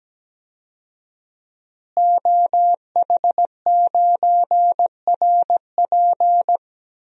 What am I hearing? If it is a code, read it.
Morse code, heard as a keyed tone, about 17 wpm, 700 Hz. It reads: OH9RP
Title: DXCC-100 Call Signs - 5 Characters - Repeated 2x - 17wpm